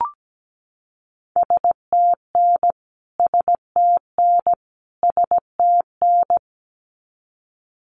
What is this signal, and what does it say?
Morse code, heard as a keyed tone, about 17 wpm, 700 Hz. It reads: STN STN STN
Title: Abbreviations Repeated 3x - 17wpm